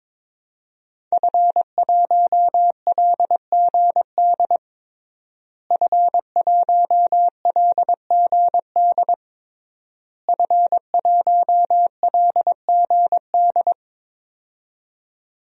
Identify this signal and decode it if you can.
Morse code, heard as a keyed tone, about 22 wpm, 700 Hz. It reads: F1LGD F1LGD F1LGD
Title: DXCC-100 Call Signs - 5 Characters - Repeated 3x - 22wpm